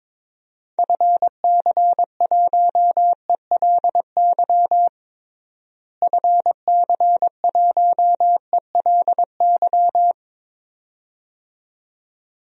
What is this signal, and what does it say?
Morse code, heard as a keyed tone, about 22 wpm, 700 Hz. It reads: FC1ELY FC1ELY
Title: DXCC-100 Call Signs - 6 Characters - Repeated 2x - 22wpm